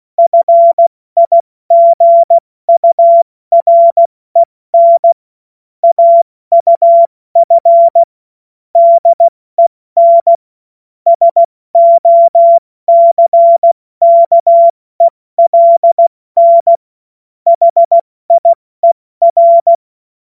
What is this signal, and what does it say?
Morse code, heard as a keyed tone, about 16 wpm, 680 Hz. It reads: FIGUREN AUF DEN SOCKELN HIER
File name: Morsekurs / Witze / Witz 18. Januar 2022, 70 F80